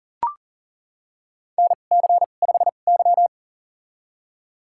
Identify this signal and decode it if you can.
Morse code, heard as a keyed tone, about 38 wpm, spaced as for 28 wpm, 700 Hz. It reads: NC5Y
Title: USA Call Signs - 4 Characters - 2x Character Spacing - 40wpm